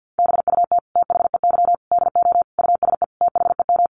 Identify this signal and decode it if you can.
Morse code, heard as a keyed tone, about 50 wpm, 700 Hz. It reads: BUT THEY DO USE THEM
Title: Sentences from Top 1300 Words 50wpm